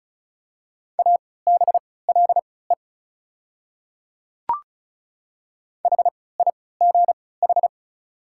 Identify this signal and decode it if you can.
Morse code, heard as a keyed tone, about 34 wpm, spaced as for 20 wpm, 700 Hz. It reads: ABLE HIGH
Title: 3x Character Spacing - Top 500 Words - 35wpm